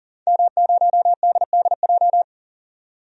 Morse code, heard as a keyed tone, about 40 wpm, 700 Hz. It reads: M0DDJ
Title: DXCC-100 Call Signs - 5 Characters - 40wpm